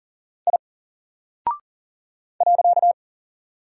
Morse code, heard as a keyed tone, about 40 wpm, 700 Hz. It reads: I .
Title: Single Character (Letter:Number:Punctuation:Prosign) Alt 40wpm